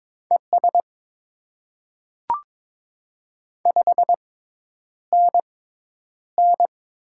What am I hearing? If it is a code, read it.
Morse code, heard as a keyed tone, about 22 wpm, spaced as for 10 wpm, 700 Hz. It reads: ES 5NN
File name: Top 20 QSO Elements - 6x Character Spacing - 22wpm